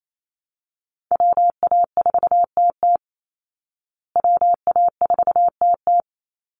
Morse code, heard as a keyed tone, about 28 wpm, 700 Hz. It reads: WA4TT WA4TT
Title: Call Signs Repeated 2x - 28wpm